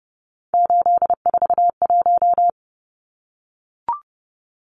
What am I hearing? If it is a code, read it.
Morse code, heard as a keyed tone, about 30 wpm, 700 Hz. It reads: 841
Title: Numbers - 3-Digit 30wpm